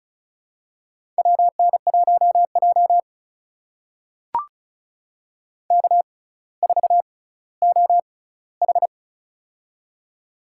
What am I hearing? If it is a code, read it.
Morse code, heard as a keyed tone, about 36 wpm, spaced as for 18 wpm, 700 Hz. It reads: WN1J K4OH
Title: USA Call Signs - 4 Characters - 6x Character Spacing - 35wpm